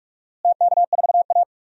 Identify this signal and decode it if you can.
Morse code, heard as a keyed tone, about 46 wpm, 700 Hz. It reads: TK4A
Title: DXCC-100 Call Signs - 4 Characters - Repeated 2x - 45wpm